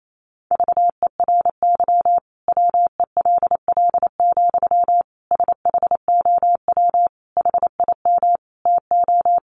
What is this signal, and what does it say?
Morse code, heard as a keyed tone, about 28 wpm, 700 Hz. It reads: VERY WELL, SHOW HIM TO
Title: Sentences from Top 200 Words 28wpm